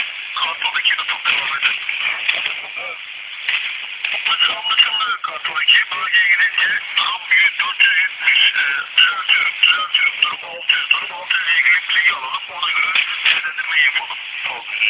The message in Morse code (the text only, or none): none